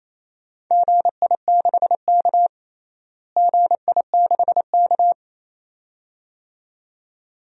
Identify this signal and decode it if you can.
Morse code, heard as a keyed tone, about 28 wpm, 700 Hz. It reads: GI6K GI6K
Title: DXCC-100 Call Signs - 4 Characters - Repeated 2x - 28wpm